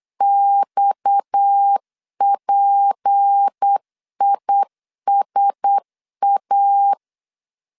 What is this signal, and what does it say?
Morse code, heard as a keyed tone, about 8 wpm, 800 Hz. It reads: XPISA
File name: LEZIONI / 6 PKX0 / 6 PKX 8wpm 3041